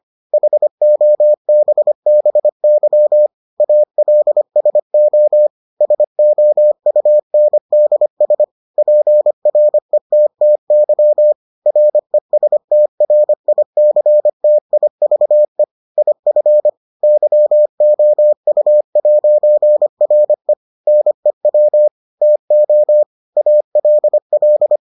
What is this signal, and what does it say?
Morse code, heard as a keyed tone, about 25 wpm, 600 Hz. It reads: HOBBY ALSO SOUNDS PRETTY RESTRICTIVE IF YOU'RE NEW TO ALL